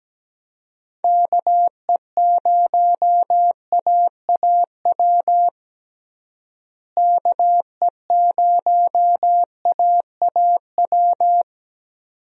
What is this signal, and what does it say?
Morse code, heard as a keyed tone, about 17 wpm, 700 Hz. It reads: KE0AAW KE0AAW
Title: USA Call Signs - 6 Characters - Repeated 2x - 17wpm